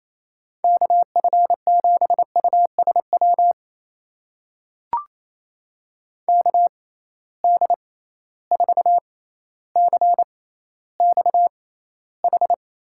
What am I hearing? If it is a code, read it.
Morse code, heard as a keyed tone, about 28 wpm, spaced as for 14 wpm, 700 Hz. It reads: KF7USW KD4CXH
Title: USA Call Signs - 6 Characters - 6x Character Spacing - 28wpm